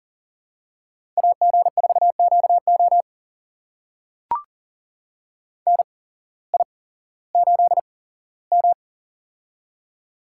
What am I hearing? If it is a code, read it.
Morse code, heard as a keyed tone, about 40 wpm, spaced as for 19 wpm, 700 Hz. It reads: AG4QO NI8M
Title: USA Call Signs - 8x Character Spacing - 40wpm